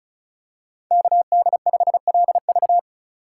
Morse code, heard as a keyed tone, about 36 wpm, 700 Hz. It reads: KD5LV
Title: USA Call Signs - 5 Characters - 3x Character Spacing - 35wpm